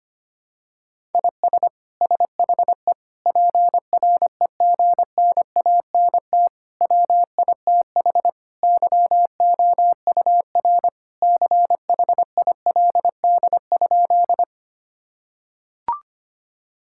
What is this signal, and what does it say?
Morse code, heard as a keyed tone, about 25 wpm, 700 Hz. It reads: IS SHE PREGNANT WITH YOUR CHILD?
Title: Sentences from Top 1000 Words - Review Part 6 - 25wpm